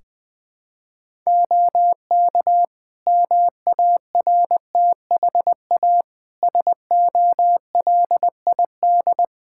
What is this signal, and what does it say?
Morse code, heard as a keyed tone, about 20 wpm, 700 Hz. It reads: OK MARTHA SOLID
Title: Practice QSO - Exchange 3 - More Info - 20wpm